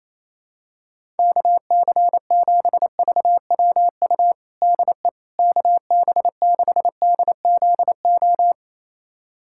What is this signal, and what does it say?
Morse code, heard as a keyed tone, about 28 wpm, 700 Hz. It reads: KC7VWU DE KB6DZO
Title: USA Callsign DE Callsign 28wpm